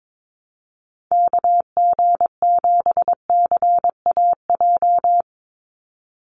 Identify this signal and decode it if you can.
Morse code, heard as a keyed tone, about 22 wpm, 700 Hz. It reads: KG7CAJ
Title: USA Call Signs 22wpm